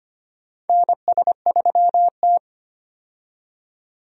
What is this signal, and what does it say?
Morse code, heard as a keyed tone, about 25 wpm, 700 Hz. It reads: NS3T
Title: USA Call Signs - 4 Characters - 25wpm